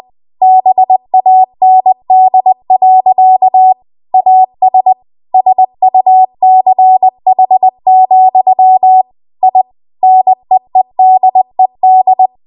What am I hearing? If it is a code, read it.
Morse code, heard as a keyed tone, about 20 wpm, 750 Hz. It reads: BAND. AS SUCH, I NEEDED
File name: Archive / 20 WPM / 20WPM